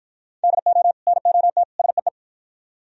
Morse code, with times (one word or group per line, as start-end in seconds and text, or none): 0.4-0.9 DO
1.1-1.6 NOT
1.8-2.1 SEE